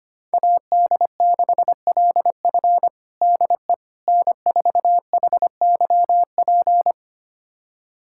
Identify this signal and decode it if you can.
Morse code, heard as a keyed tone, about 25 wpm, 700 Hz. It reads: AD6LF DE N4HYP
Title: USA Callsign DE Callsign - 5 Characters - 25wpm